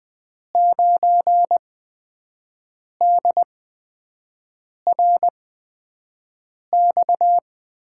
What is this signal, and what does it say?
Morse code, heard as a keyed tone, about 20 wpm, spaced as for 5 wpm, 700 Hz. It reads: 9DRX
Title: USA Call Signs - 8x Character Spacing - 20wpm